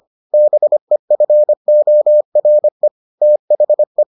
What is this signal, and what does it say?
Morse code, heard as a keyed tone, about 25 wpm, 600 Hz. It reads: BEFORE THE